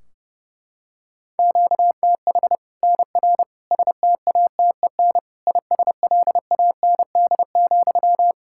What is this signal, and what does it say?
Morse code, heard as a keyed tone, about 32 wpm, 700 Hz. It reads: QTH NR STATEN ISLAND,
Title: Practice QSO - Exchange 1 - Beginning - 30wpm